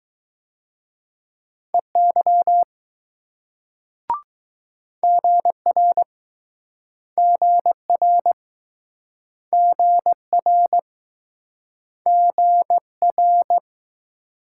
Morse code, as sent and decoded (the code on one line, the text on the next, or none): . -.-- / --. .-. / --. .-. / --. .-. / --. .-.
EY GR GR GR GR